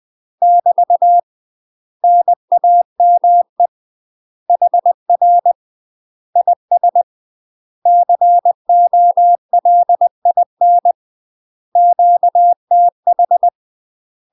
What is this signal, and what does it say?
Morse code, heard as a keyed tone, about 20 wpm, 700 Hz. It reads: <BT> NAME HR IS COLIN QTH
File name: UK MASTER QSOs TRAINING / 20WPM / UK MASTER QSOs 0011